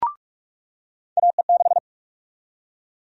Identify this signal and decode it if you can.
Morse code, heard as a keyed tone, about 42 wpm, 700 Hz. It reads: AE6
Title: USA Call Sign Prefixes - 45wpm